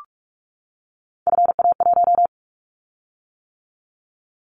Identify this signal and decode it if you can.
Morse code, heard as a keyed tone, about 44 wpm, 700 Hz. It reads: FA1
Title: Callsign Prefixes for Top 50 DXCC - 45wpm